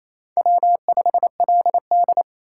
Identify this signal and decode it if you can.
Morse code, heard as a keyed tone, about 28 wpm, 700 Hz. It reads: W5LD
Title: USA Call Signs - 4 Characters - 2x Character Spacing - 28wpm